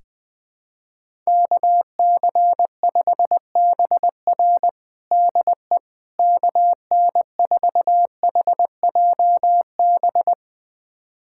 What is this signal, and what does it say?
Morse code, heard as a keyed tone, about 20 wpm, 700 Hz. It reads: KC5BR DE KN4HJB
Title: USA Callsign DE Callsign 20wpm